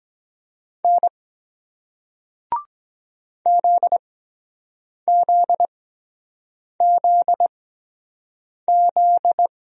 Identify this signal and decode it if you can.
Morse code, heard as a keyed tone, about 24 wpm, spaced as for 7 wpm, 700 Hz. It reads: N ZZZZ